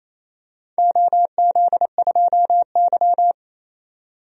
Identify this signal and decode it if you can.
Morse code, heard as a keyed tone, about 28 wpm, 700 Hz. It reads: OZ2Y
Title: DXCC-100 Call Signs - 4 Characters - 28wpm